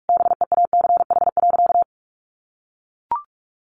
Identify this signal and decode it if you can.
Morse code, heard as a keyed tone, about 44 wpm, 700 Hz. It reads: BEACH.
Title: Sentences from Top 1500 Words 45wpm